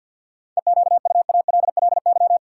Morse code, heard as a keyed tone, about 51 wpm, 700 Hz. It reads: EQUALLY
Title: Sentences from Top 1400 Words - New Word Drill Alt - 50wpm